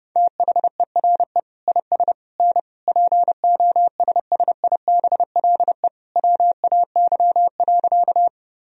Morse code, heard as a keyed tone, about 32 wpm, 700 Hz. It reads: THERE IS N POSSIBLE WAY.
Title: Sentences from Top 500 Words and Abbreviations 30wpm